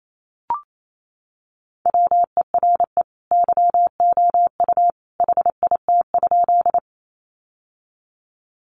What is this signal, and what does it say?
Morse code, heard as a keyed tone, about 28 wpm, 700 Hz. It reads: WERE YOU HIT?
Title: Sentences from Top 1200 Words 28wpm